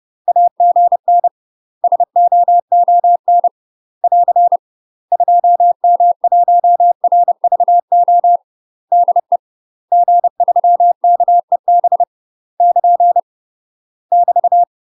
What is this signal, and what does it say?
Morse code, heard as a keyed tone, about 30 wpm, 700 Hz. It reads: AGN SOON <AR> 2M1RVO DE G3KEB <KN> <BT>
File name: UK MASTER QSOs TRAINING/30WPM